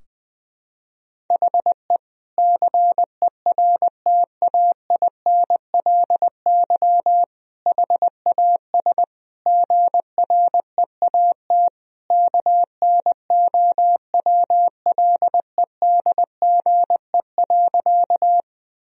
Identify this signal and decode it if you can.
Morse code, heard as a keyed tone, about 20 wpm, 700 Hz. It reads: HE CERTAINLY HAS GREAT KNOWLEDGE.
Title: Sentences from Top 1000 Words - Review Part 3 - 20wpm